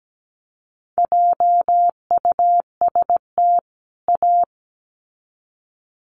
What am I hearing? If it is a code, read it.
Morse code, heard as a keyed tone, about 17 wpm, 700 Hz. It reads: JUST A
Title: Sets of 2 Words 17wpm